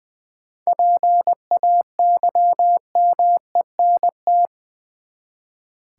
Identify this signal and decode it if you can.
Morse code, heard as a keyed tone, about 20 wpm, 700 Hz. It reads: PAYMENT
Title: Seven Letter Words - 20wpm